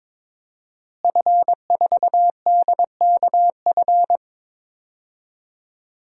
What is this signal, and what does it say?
Morse code, heard as a keyed tone, about 22 wpm, 700 Hz. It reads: F4DKF
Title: DXCC-100 Call Signs - 5 Characters - Repeated 2x - 22wpm